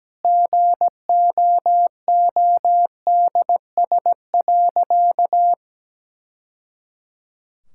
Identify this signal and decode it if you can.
Morse code, heard as a keyed tone, about 17 wpm, 700 Hz. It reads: GOODS.
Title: Sentences from Top 1000 Words - Review Part 6 - 17wpm